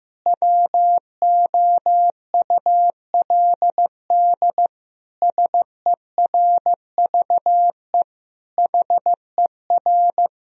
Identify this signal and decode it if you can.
Morse code, heard as a keyed tone, about 16 wpm, 700 Hz. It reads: WOULD SERVE HER
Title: Sentences from Top 1000 Words - Review Part 2 - 15wpm